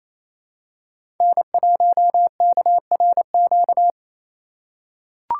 Morse code, heard as a keyed tone, about 27 wpm, 700 Hz. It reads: N1KRQ
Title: USA Call Signs - 4x Character Spacing - 28wpm